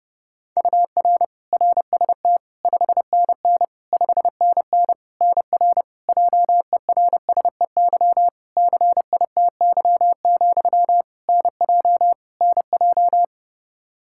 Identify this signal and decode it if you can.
Morse code, heard as a keyed tone, about 32 wpm, 700 Hz. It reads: UR RST 5NN 5NN NR JERSEY CITY, NJ NJ
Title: Practice QSO - Exchange 1 - Beginning - 30wpm